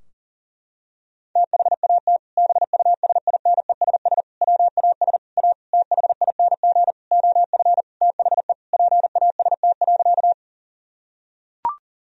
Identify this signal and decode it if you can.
Morse code, heard as a keyed tone, about 37 wpm, 700 Hz. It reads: THAT BUSINESS WAS A THING OF THE PAST.
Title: Sentences from Top 1000 Words - Review Part 3 - 40wpm